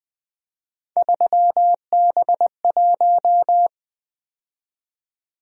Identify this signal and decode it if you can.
Morse code, heard as a keyed tone, about 20 wpm, 700 Hz. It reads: 3B1
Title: Three Letter-Number Combinations - 20wpm